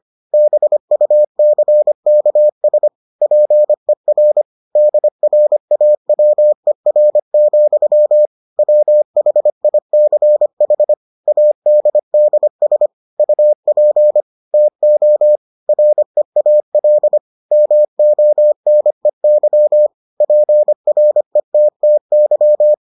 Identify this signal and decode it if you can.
Morse code, heard as a keyed tone, about 25 wpm, 600 Hz. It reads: BUCKS PER DRAWER, WHICH ADDS UP TO REAL MONEY PRETTY